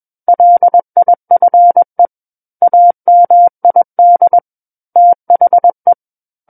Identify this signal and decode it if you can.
Morse code, heard as a keyed tone, about 21 wpm, 700 Hz. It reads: LIFE AMID THE
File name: Book One / 12-Chapter 12 - 21 WPM